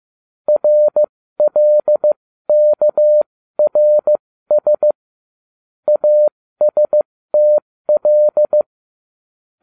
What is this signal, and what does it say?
Morse code, heard as a keyed tone, about 15 wpm, spaced as for 13 wpm, 600 Hz. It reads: RLKRS ASTL